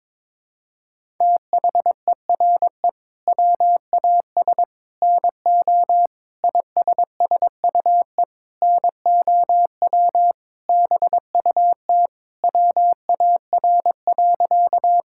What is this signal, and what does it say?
Morse code, heard as a keyed tone, about 22 wpm, 700 Hz. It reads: THERE WAS NO ISSUE NOW BUT WAR.
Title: Sentences from Top 1000 Words - Review Part 6 - 22wpm